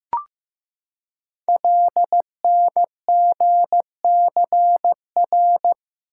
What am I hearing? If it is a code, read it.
Morse code, heard as a keyed tone, about 15 wpm, 700 Hz. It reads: LNGCR
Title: Code Groups - 5 Letters - 15wpm